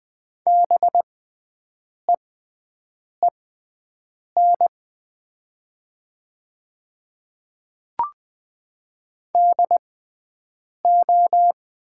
Morse code, heard as a keyed tone, about 20 wpm, spaced as for 7 wpm, 700 Hz. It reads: BEEN DO
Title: 6x Character Spacing - Top 100 Words - 20wpm